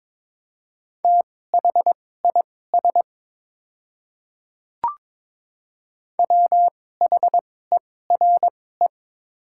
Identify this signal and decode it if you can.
Morse code, heard as a keyed tone, about 22 wpm, spaced as for 16 wpm, 700 Hz. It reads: THIS WHERE
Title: 2x Character Spacing - Top 300 Words - 22wpm